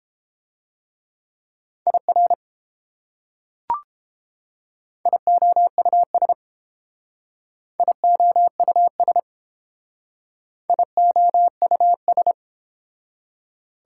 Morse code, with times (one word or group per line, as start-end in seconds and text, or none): 1.9-2.3 IR
5.1-6.3 IOUS
7.8-9.2 IOUS
10.7-12.3 IOUS